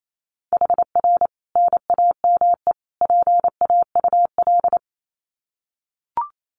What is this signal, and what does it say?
Morse code, heard as a keyed tone, about 28 wpm, 700 Hz. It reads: HR NAME PAUL
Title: Common QSO Phrases 28wpm